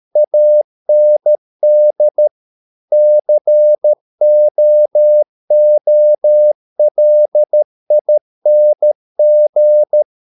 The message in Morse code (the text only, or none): AND COOLING